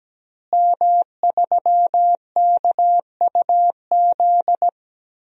Morse code, heard as a keyed tone, about 17 wpm, 700 Hz. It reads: M3KUZ
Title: DXCC-100 Call Signs - 5 Characters - Repeated 2x - 17wpm